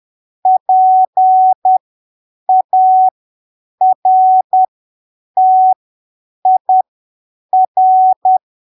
Morse code, heard as a keyed tone, about 10 wpm, spaced as for 7 wpm, 750 Hz. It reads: PARTIR
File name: arquivos / texto / texto 15 10ppm